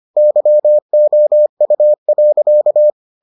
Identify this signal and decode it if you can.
Morse code, heard as a keyed tone, about 25 wpm, 600 Hz. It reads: YOU.